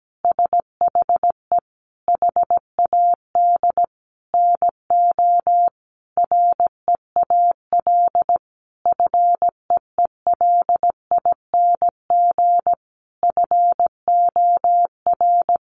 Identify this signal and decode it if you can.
Morse code, heard as a keyed tone, about 17 wpm, 700 Hz. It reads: SHE HAD NO REAL FEELING FOR
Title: Sentences from Top 1200 Words 17wpm